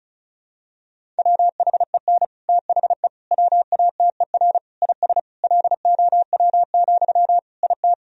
Morse code, heard as a keyed tone, about 36 wpm, 700 Hz. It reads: WHEN THE WATER IS LOW, IT
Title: Sentences from Top 1000 Words - Review Part 1 - 35wpm